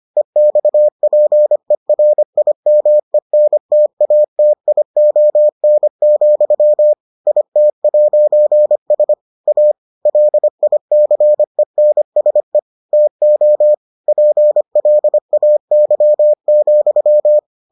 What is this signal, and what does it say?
Morse code, heard as a keyed tone, about 25 wpm, 600 Hz. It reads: EXPERIMENTATION, IT'S A LICENSE TO PLAY,